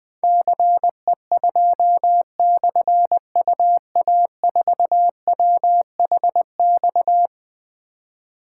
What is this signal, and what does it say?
Morse code, heard as a keyed tone, about 20 wpm, 700 Hz. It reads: CE2/UA4WHX
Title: Difficult Call Signs 20wpm